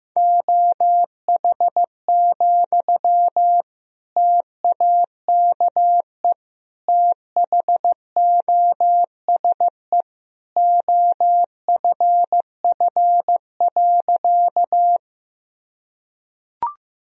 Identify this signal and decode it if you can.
Morse code, heard as a keyed tone, about 15 wpm, 700 Hz. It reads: OH, TAKE THOSE OFF.
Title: Sentences from Top 1000 Words - Review Part 1 - 15wpm